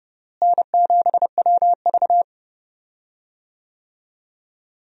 Morse code, heard as a keyed tone, about 30 wpm, 700 Hz. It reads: N7WV
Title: USA Call Signs - 4 Characters - Repeated 2x - 30wpm